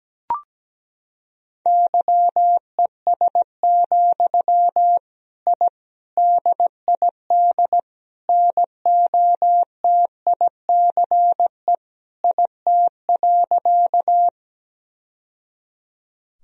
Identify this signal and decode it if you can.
Morse code, heard as a keyed tone, about 17 wpm, 700 Hz. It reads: YES, I DID NOTICE IT.
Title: Sentences from Top 1000 Words - Review Part 2 - 17wpm